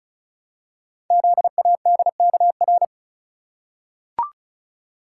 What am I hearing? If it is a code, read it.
Morse code, heard as a keyed tone, about 36 wpm, 700 Hz. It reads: ZADKR
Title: Code Groups - 5 Letters - 35wpm